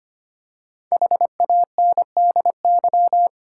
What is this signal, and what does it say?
Morse code, heard as a keyed tone, about 25 wpm, 700 Hz. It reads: HANDY
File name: 5 Letter Words - First 3 Letters Given - 25wpm